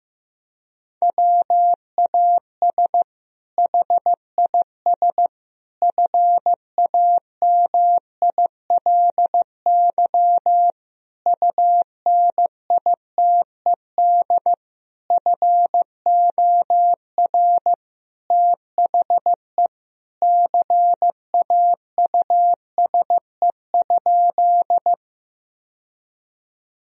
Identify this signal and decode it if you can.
Morse code, heard as a keyed tone, about 15 wpm, 700 Hz. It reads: WAS HIS FAMILY UNITED FOR THE CAUSE?
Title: Sentences from Top 1000 Words - Review Part 4 - 15wpm